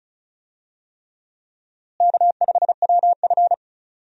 Morse code, heard as a keyed tone, about 36 wpm, 700 Hz. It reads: K5WF
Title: USA Call Signs - 4 Characters - Repeated 3x - 35wpm